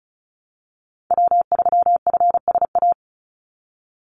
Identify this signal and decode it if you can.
Morse code, heard as a keyed tone, about 34 wpm, 700 Hz. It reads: W3FSA
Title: Call Signs - 35wpm